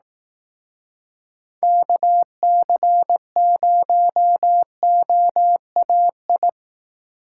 Morse code, heard as a keyed tone, about 18 wpm, 700 Hz. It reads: KC0OAI
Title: Speed Racing - USA 2x3 Call Signs 15wpm